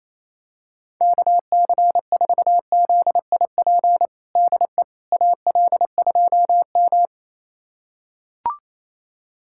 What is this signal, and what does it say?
Morse code, heard as a keyed tone, about 28 wpm, 700 Hz. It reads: KC4ZIP DE AL2M
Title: USA Callsign DE Callsign 28wpm